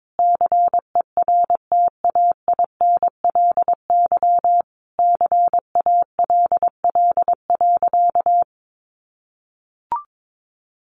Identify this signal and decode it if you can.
Morse code, heard as a keyed tone, about 22 wpm, 700 Hz. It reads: CERTAINLY CALL.
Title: Sentences from Top 900 Words 22wpm